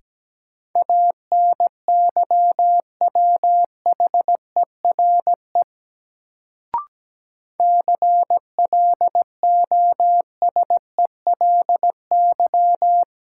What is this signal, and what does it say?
Morse code, heard as a keyed tone, about 17 wpm, 700 Hz. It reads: ANYWHERE CLOSELY
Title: Sentences from Top 1400 Words - New Word Drill Alt - 17wpm